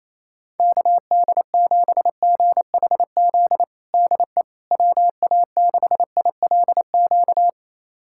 Morse code, heard as a keyed tone, about 28 wpm, 700 Hz. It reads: KD7GHZ DE WA6ILQ
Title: USA Callsign DE Callsign - 6 Characters - 28wpm